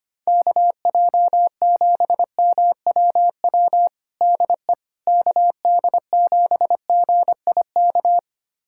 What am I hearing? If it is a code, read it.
Morse code, heard as a keyed tone, about 25 wpm, 700 Hz. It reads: KJ7MWW DE KD7GIK
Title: USA Callsign DE Callsign - 6 Characters - 25wpm